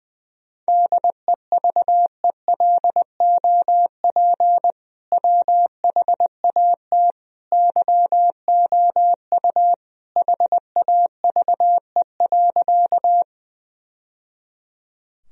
Morse code, heard as a keyed tone, about 21 wpm, 700 Hz. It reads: DEVELOP WHAT YOU HAVE.
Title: Sentences from Top 1000 Words - Review Part 2 - 20wpm